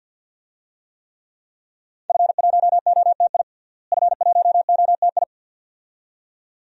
Morse code, heard as a keyed tone, about 47 wpm, 700 Hz. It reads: F1OTI F1OTI
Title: DXCC-100 Call Signs - 5 Characters - Repeated 2x - 50wpm